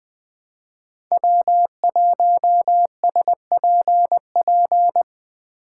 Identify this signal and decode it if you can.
Morse code, heard as a keyed tone, about 20 wpm, 700 Hz. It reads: W1SPP
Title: USA Call Signs - 5 Characters - 2x Character Spacing - 20wpm